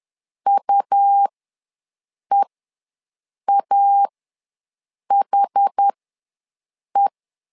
Morse code, heard as a keyed tone, about 10 wpm, spaced as for 6 wpm, 800 Hz. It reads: UEAHE
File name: LEZIONI / 2 UHDN / 2 UH ws 10wpm 2029